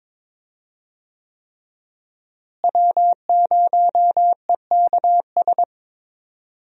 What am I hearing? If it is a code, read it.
Morse code, heard as a keyed tone, about 22 wpm, 700 Hz. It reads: W0EKS
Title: USA Call Signs - 5 Characters - Repeated 3x - 22wpm